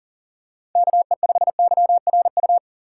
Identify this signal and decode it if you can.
Morse code, heard as a keyed tone, about 40 wpm, 700 Hz. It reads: KE5YRU
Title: USA Call Signs - 6 Characters - 3x Character Spacing - 40wpm